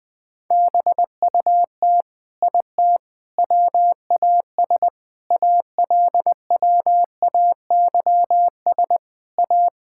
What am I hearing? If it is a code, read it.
Morse code, heard as a keyed tone, about 20 wpm, 700 Hz. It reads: BUT IT WAS ALWAYS A
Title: Sentences from Top 1600 Words 20wpm